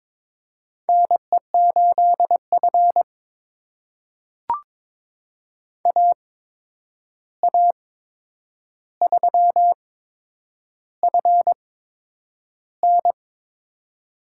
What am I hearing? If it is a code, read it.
Morse code, heard as a keyed tone, about 22 wpm, spaced as for 9 wpm, 700 Hz. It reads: NE8F AA3FN
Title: USA Call Signs - 8x Character Spacing - 22wpm